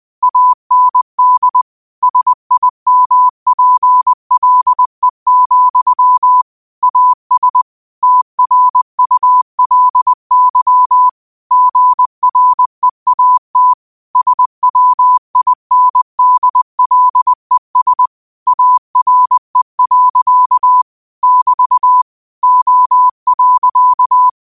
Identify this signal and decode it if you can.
Morse code, heard as a keyed tone, about 20 wpm, 1000 Hz. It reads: AND SIMPLE, AS TRULY GREAT SWINDLES ARE. <BT> O.
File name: qotdfiles / 20wpm